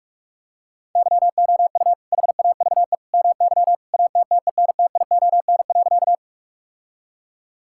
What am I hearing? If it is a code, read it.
Morse code, heard as a keyed tone, about 46 wpm, 700 Hz. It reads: YOU HAVE MY ATTENTION.
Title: Sentences from Top 1000 Words - Review Part 6 - 45wpm